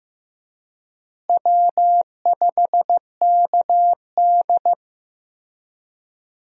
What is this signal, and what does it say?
Morse code, heard as a keyed tone, about 15 wpm, 700 Hz. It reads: W5KD
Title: USA Call Signs - 4 Characters - Repeated 3x - 15wpm